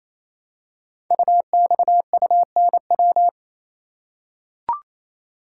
Morse code, heard as a keyed tone, about 27 wpm, 700 Hz. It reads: UXUNW
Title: Code Groups - 5 Letters - 28wpm